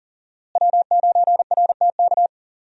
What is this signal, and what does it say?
Morse code, heard as a keyed tone, about 40 wpm, 700 Hz. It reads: W9RTK
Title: USA Call Signs - 5 Characters - Repeated 2x - 40wpm